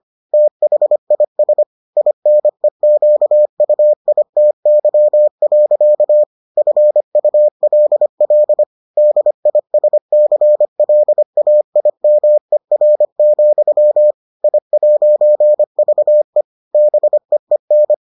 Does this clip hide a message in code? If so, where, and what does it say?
Yes: Morse code from 0.3 to 17.9 s, heard as a keyed tone, about 25 wpm, 600 Hz. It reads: THIS INEQUITY. FULL DISCLAIMER, I'VE BEEN